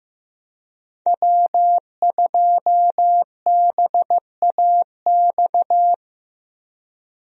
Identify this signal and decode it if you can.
Morse code, heard as a keyed tone, about 15 wpm, 700 Hz. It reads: W2BAX
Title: USA Call Signs - 5 Characters - Repeated 2x - 15wpm